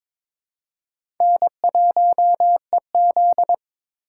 Morse code, heard as a keyed tone, about 21 wpm, 700 Hz. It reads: N1EZ
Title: USA Call Signs - 4 Characters - 3x Character Spacing - 22wpm